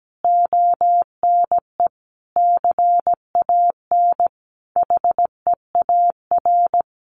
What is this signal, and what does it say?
Morse code, heard as a keyed tone, about 17 wpm, 700 Hz. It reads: ONE CAN HEAR
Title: Sentences from Top 600 Words 17wpm